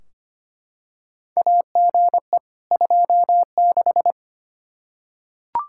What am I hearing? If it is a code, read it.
Morse code, heard as a keyed tone, about 25 wpm, 700 Hz. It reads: AGE 26
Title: Practice QSO - Exchange 3 - More Info - 25wpm